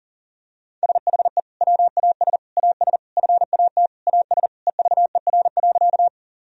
Morse code, heard as a keyed tone, about 40 wpm, 690 Hz. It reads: SHE WAS AS FAT AS EVER.
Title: Sentences from Top 1000 Words - Review Part 4 - 40wpm